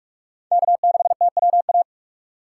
Morse code, heard as a keyed tone, about 47 wpm, 700 Hz. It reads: K6TWA
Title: USA Call Signs - 5 Characters - 4x Character Spacing - 45wpm